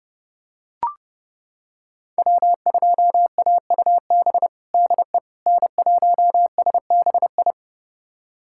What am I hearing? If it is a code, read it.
Morse code, heard as a keyed tone, about 30 wpm, 700 Hz. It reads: W2AUB DE N1SBI
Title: USA Callsign DE Callsign 30wpm